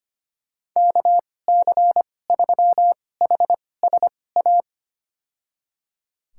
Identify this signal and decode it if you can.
Morse code, heard as a keyed tone, about 25 wpm, spaced as for 18 wpm, 700 Hz. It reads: KC3HSA
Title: USA Call Signs - 6 Characters - 2x Character Spacing - 25wpm